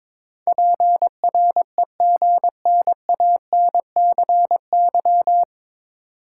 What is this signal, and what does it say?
Morse code, heard as a keyed tone, about 22 wpm, 700 Hz. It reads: PREGNANCY
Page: Morse Code Ninja practice 22wpm